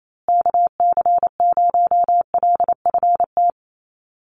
Morse code, heard as a keyed tone, about 27 wpm, 700 Hz. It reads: KC0LFT
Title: USA Call Signs Repeated 2x 28wpm